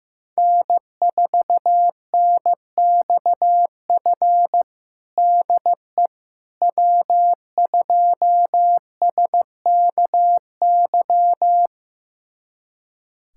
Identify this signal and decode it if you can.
Morse code, heard as a keyed tone, about 15 wpm, 700 Hz. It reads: N4NXF DE W2SKY
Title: USA Callsign DE Callsign - 5 Characters - 15wpm